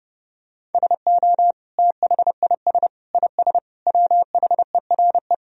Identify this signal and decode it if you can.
Morse code, heard as a keyed tone, about 30 wpm, 700 Hz. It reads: SO THIS IS WHERE
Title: Sentences from Top 1000 Words - Review Part 2 - 30wpm